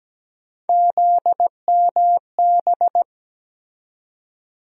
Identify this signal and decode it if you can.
Morse code, heard as a keyed tone, about 17 wpm, 700 Hz. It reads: ZMB
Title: USA Call Sign Postfixes - 3 Letters 17wpm